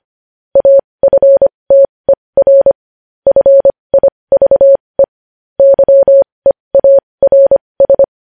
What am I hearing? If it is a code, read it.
Morse code, heard as a keyed tone, about 25 wpm, spaced as for 20 wpm, 550 Hz. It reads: AFTER FIVE YEARS
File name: podcast / News headlines-1x-25x20